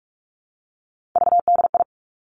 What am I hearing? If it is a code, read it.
Morse code, heard as a keyed tone, about 46 wpm, 700 Hz. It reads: VDI
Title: Three Challenging Letter-Number Combinations 45wpm